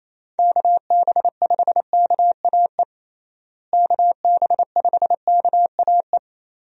Morse code, heard as a keyed tone, about 28 wpm, 700 Hz. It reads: KB5KAE KB5KAE
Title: USA Call Signs - 6 Characters - Repeated 2x - 28wpm